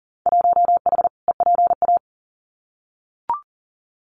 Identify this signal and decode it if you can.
Morse code, heard as a keyed tone, about 40 wpm, 700 Hz. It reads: 1H EPA
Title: Winter Field Day Exchanges Encore 40wpm